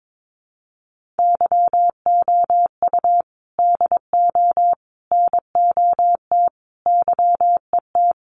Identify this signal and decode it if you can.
Morse code, heard as a keyed tone, about 22 wpm, 700 Hz. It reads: YOU DO NOT YET
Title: Sentences from Top 1500 Words 22wpm